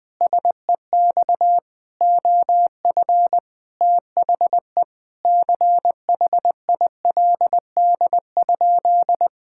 Morse code, heard as a keyed tone, about 21 wpm, 700 Hz. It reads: SEX OF THE CHILD?
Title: Sentences from Top 1600 Words 20wpm